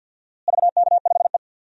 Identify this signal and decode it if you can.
Morse code, heard as a keyed tone, about 53 wpm, 700 Hz. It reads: VK5E
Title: DXCC-100 Call Signs - 4 Characters - Repeated 2x - 50wpm